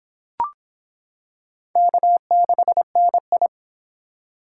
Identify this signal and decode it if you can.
Morse code, heard as a keyed tone, about 23 wpm, 700 Hz. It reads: K6NI